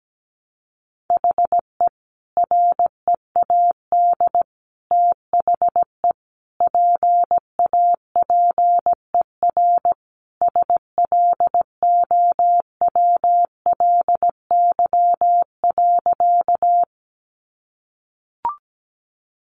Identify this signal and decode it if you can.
Morse code, heard as a keyed tone, about 17 wpm, 700 Hz. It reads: HE READ THE PAPER SLOWLY.
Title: Sentences from Top 1100 Words 17wpm